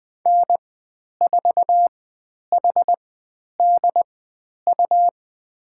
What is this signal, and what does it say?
Morse code, heard as a keyed tone, about 20 wpm, spaced as for 10 wpm, 700 Hz. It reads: N4HDU
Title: 102 Lesson 17g - Letter u - Call Signs 20wpm